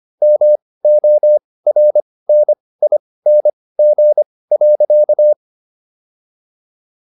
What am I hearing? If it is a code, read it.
Morse code, heard as a keyed tone, about 25 wpm, spaced as for 18 wpm, 600 Hz. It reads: MORNING.